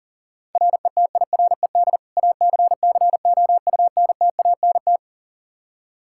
Morse code, heard as a keyed tone, about 40 wpm, 700 Hz. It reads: RETIRED ACCOUNTANT
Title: Practice QSO - Exchange 3 - More Info - 40wpm